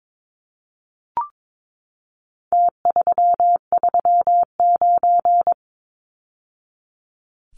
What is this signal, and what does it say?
Morse code, heard as a keyed tone, about 22 wpm, 700 Hz. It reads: T339
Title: Numbers - Contest Style - 4-Digit 22wpm